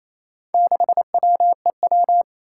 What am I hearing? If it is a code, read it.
Morse code, heard as a keyed tone, about 28 wpm, 700 Hz. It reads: 6WEW
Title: Four Challenging Letter-Number Combinations 28wpm